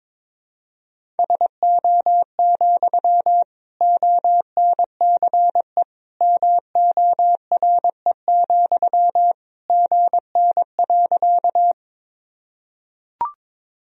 Morse code, heard as a keyed tone, about 22 wpm, 700 Hz. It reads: SO, ONCE MORE, GN.
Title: Sentences from Top 300 Words and Abbreviations 22wpm